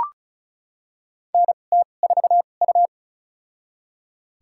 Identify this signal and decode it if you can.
Morse code, heard as a keyed tone, about 35 wpm, spaced as for 25 wpm, 700 Hz. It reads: NT4U
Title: USA Call Signs - 4 Characters - 2x Character Spacing - 35wpm